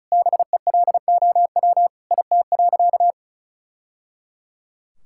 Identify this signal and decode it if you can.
Morse code, heard as a keyed tone, about 36 wpm, 700 Hz. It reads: BELOW IT.
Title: Sentences from Top 1000 Words - Review Part 5 - 35wpm